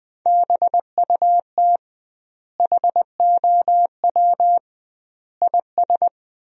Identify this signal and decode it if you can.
Morse code, heard as a keyed tone, about 21 wpm, 700 Hz. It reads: BUT HOW IS
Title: 2x Word-Spacing - Sentences from Top 100 Words - 20wpm